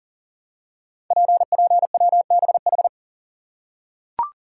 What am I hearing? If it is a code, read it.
Morse code, heard as a keyed tone, about 39 wpm, 690 Hz. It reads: PPWBH
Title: Code Groups - 5 Letters - 40wpm